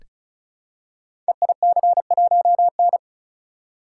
Morse code, heard as a keyed tone, about 36 wpm, 700 Hz. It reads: EIC1N
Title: Code Groups - 5 Letter:Numbers - 35wpm